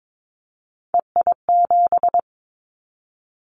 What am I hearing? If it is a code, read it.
Morse code, heard as a keyed tone, about 22 wpm, 700 Hz. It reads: EI7
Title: Callsign Prefixes for Top 50 DXCC - 22wpm